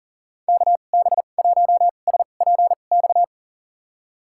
Morse code, heard as a keyed tone, about 40 wpm, spaced as for 29 wpm, 700 Hz. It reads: KB1SPX
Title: USA Call Signs - 6 Characters - 2x Character Spacing - 40wpm